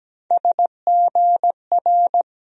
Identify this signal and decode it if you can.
Morse code, heard as a keyed tone, about 17 wpm, 700 Hz. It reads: SGR
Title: USA Call Sign Postfixes - 3 Letters 17wpm